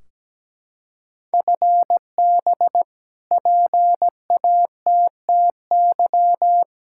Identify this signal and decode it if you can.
Morse code, heard as a keyed tone, about 17 wpm, 700 Hz. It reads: FB PATTY
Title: K1USN SST QSO Practice 17wpm